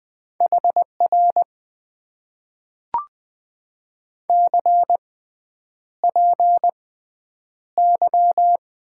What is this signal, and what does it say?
Morse code, heard as a keyed tone, about 20 wpm, spaced as for 9 wpm, 700 Hz. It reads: HR CPY
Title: Top 90 QSO Elements - 6x Character Spacing - 20wpm